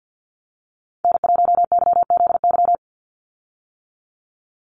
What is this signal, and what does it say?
Morse code, heard as a keyed tone, about 49 wpm, 700 Hz. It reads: N1YZY
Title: USA Call Signs 50wpm